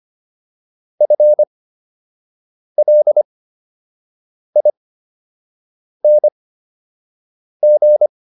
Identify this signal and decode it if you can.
Morse code, heard as a keyed tone, about 25 wpm, spaced as for 6 wpm, 600 Hz. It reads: FLING